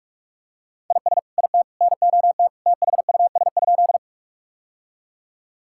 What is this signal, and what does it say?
Morse code, heard as a keyed tone, about 41 wpm, 700 Hz. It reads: IS IT NOT THUS?
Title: Sentences from Top 1000 Words - Review Part 5 - 45wpm